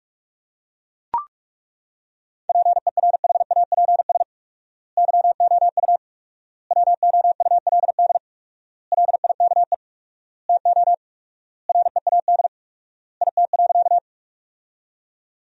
Morse code, heard as a keyed tone, about 42 wpm, 700 Hz. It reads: PERHAPS YOU WOULD LIKE TO READ IT.
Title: 4x Word-Spacing - Sentences from Top 500 Words - 45wpm